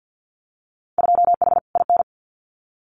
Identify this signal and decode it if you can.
Morse code, heard as a keyed tone, about 50 wpm, 700 Hz. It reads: 2H IN
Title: Winter Field Day Exchanges Encore 50wpm